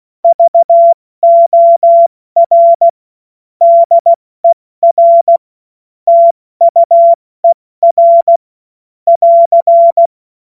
Morse code, heard as a keyed tone, about 16 wpm, 680 Hz. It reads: VOR DER TUER <AR>
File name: Morsekurs / Witze / Witz 24. Januar 2023 70 F80